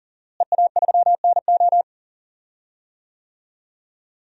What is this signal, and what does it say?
Morse code, heard as a keyed tone, about 38 wpm, 700 Hz. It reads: EA3NO
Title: DXCC-100 Call Signs - 5 Characters - Repeated 2x - 40wpm